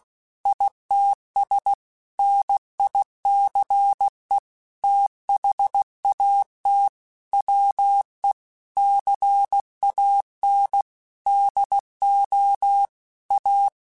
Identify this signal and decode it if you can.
Morse code, heard as a keyed tone, about 16 wpm, 800 Hz. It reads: ITS NICE THAT WE CAN DO A
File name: QSOs / QSOs-mp3s / WD0BGP 2@15